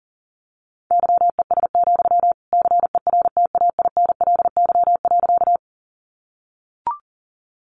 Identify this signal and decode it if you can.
Morse code, heard as a keyed tone, about 40 wpm, 700 Hz. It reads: YES, CERTAINLY.
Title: Sentences from Top 900 Words 40wpm